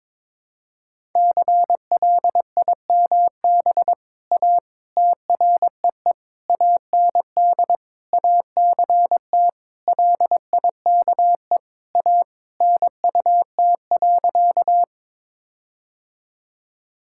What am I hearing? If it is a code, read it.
Morse code, heard as a keyed tone, about 22 wpm, 700 Hz. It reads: CLIMB A TREE AND ACT LIKE A NUT.
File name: All-Age Appropriate Jokes - 22wpm